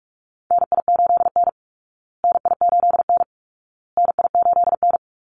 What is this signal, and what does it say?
Morse code, heard as a keyed tone, about 44 wpm, 700 Hz. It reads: NI8N NI8N NI8N
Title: Call Signs Repeated 2x - 45wpm